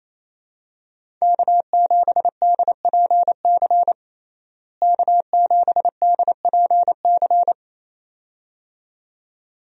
Morse code, heard as a keyed tone, about 27 wpm, 700 Hz. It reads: K7DPC K7DPC
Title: USA Call Signs - 5 Characters - Repeated 2x - 28wpm